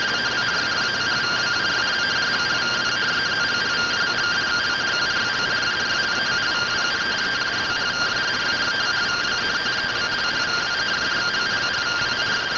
Two-tone teleprinter signal, about 170 Hz shift.